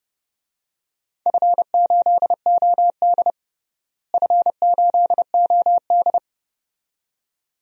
Morse code, heard as a keyed tone, about 31 wpm, 700 Hz. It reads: F8OD F8OD
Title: DXCC-100 Call Signs - 4 Characters - Repeated 2x - 30wpm